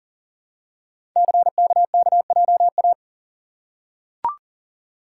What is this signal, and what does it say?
Morse code, heard as a keyed tone, about 42 wpm, 690 Hz. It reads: CKKJA
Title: Code Groups - 5 Letters - 40wpm